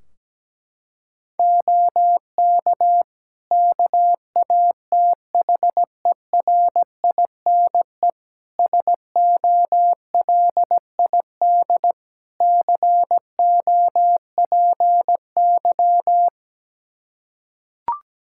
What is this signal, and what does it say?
Morse code, heard as a keyed tone, about 17 wpm, 700 Hz. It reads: OK KATHERINE SOLID COPY
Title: Practice QSO - Exchange 3 - More Info - 17wpm